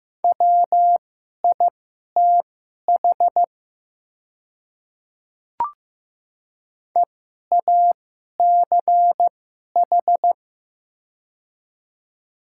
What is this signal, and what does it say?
Morse code, heard as a keyed tone, about 15 wpm, spaced as for 11 wpm, 700 Hz. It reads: WITH EACH
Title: 2x Character Spacing - Top 100 Words - 15wpm